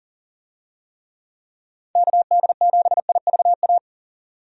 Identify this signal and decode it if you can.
Morse code, heard as a keyed tone, about 40 wpm, 700 Hz. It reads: KD7IVA
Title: USA Call Signs - 6 Characters - 40wpm